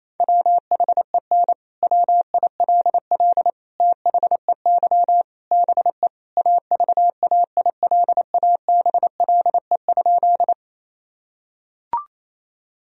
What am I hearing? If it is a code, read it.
Morse code, heard as a keyed tone, about 28 wpm, 700 Hz. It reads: WHEN WILL THEY BE AVAILABLE?
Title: Sentences from Top 1000 Words - Review Part 3 - 28wpm